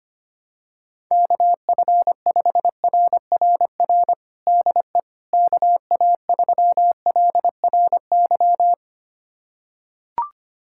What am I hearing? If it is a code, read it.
Morse code, heard as a keyed tone, about 25 wpm, 700 Hz. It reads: KF5RRR DE KA3LRY
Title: USA Callsign DE Callsign - 6 Characters - 25wpm